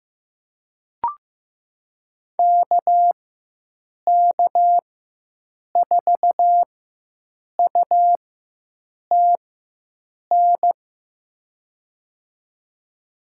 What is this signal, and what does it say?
Morse code, heard as a keyed tone, about 15 wpm, spaced as for 7 wpm, 700 Hz. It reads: KK4UTN
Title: USA Call Signs - 4x Character Spacing - 15wpm